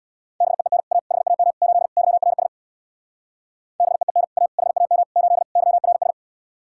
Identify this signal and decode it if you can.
Morse code, heard as a keyed tone, about 64 wpm, 700 Hz. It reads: BEEN A HAM 8 YRS BEEN A HAM 8 YRS